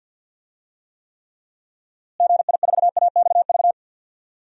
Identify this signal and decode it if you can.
Morse code, heard as a keyed tone, about 51 wpm, 700 Hz. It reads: GI4AXV
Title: DXCC-100 Call Signs - 6 Characters - Repeated 3x - 50wpm